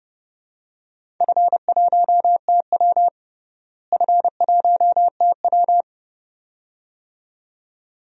Morse code, heard as a keyed tone, about 31 wpm, 700 Hz. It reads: F1TW F1TW
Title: DXCC-100 Call Signs - 4 Characters - Repeated 2x - 30wpm